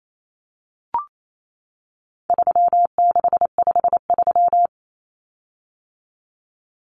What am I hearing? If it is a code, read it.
Morse code, heard as a keyed tone, about 28 wpm, 700 Hz. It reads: 3653
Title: Common Cut Numbers - 4-Digit 28wpm